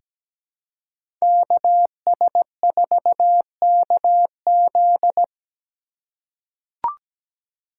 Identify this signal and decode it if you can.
Morse code, heard as a keyed tone, about 17 wpm, 700 Hz. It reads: KS4KZ
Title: USA Call Signs - 5 Characters - 3x Character Spacing - 17wpm